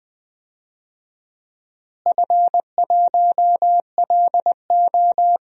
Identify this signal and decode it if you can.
Morse code, heard as a keyed tone, about 20 wpm, 700 Hz. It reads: F1LO
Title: DXCC-100 Call Signs - 4 Characters - Repeated 2x - 20wpm